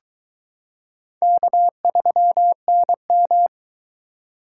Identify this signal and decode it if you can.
Morse code, heard as a keyed tone, about 21 wpm, 700 Hz. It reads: K3NM